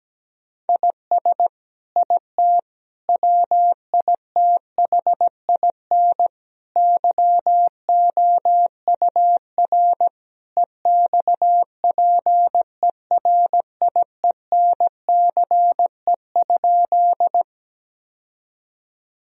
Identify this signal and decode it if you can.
Morse code, heard as a keyed tone, about 17 wpm, 700 Hz. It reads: IS IT WITHIN YOUR EXPERIENCE?
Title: Sentences from Top 1000 Words - Review Part 4 - 17wpm